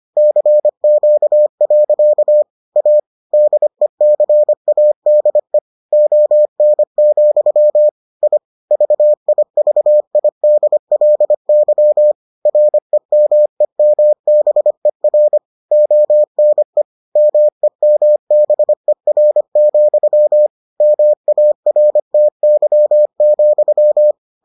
Morse code, heard as a keyed tone, about 25 wpm, 600 Hz. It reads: CQ. A DECADE ON, I VIVIDLY REMEMBER ONE MEMBER, MARTY,